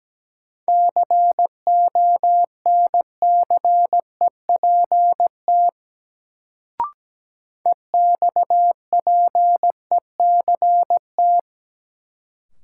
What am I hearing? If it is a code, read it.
Morse code, heard as a keyed tone, about 17 wpm, 700 Hz. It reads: CONCEPT EXPECT
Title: Sentences from Top 1100 Words - New Word Drill Alt - 17wpm